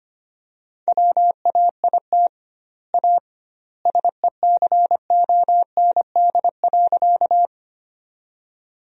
Morse code, heard as a keyed tone, about 25 wpm, 700 Hz. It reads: WAIT A SECOND.
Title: 2x Word-Spacing - Sentences from Top 500 Words - 25wpm